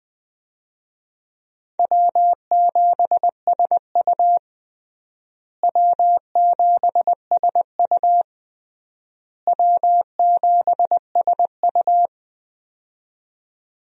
Morse code, heard as a keyed tone, about 20 wpm, 700 Hz. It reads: W7SU W7SU W7SU
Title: USA Call Signs - 4 Characters - Repeated 3x - 20wpm